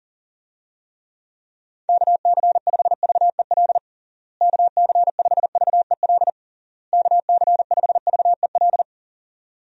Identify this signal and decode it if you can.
Morse code, heard as a keyed tone, about 40 wpm, 700 Hz. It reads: KC5VEL KC5VEL KC5VEL
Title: USA Call Signs - 6 Characters - Repeated 3x - 40wpm